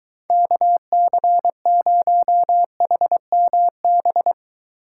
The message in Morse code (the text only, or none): KC0HMB